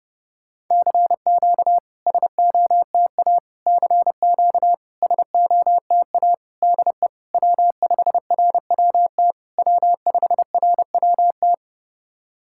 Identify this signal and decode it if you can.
Morse code, heard as a keyed tone, about 32 wpm, 700 Hz. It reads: CQ SOTA CQ SOTA DE W5RWT W5RWT
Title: SOTA QSO Practice 30wpm